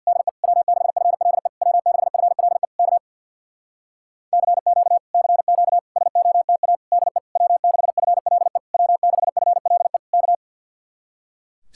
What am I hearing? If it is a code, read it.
Morse code, heard as a keyed tone, about 65 wpm, 700 Hz. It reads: DE W6FLE W6FLE K CQ CQ SOTA DE W6FLE W6FLE K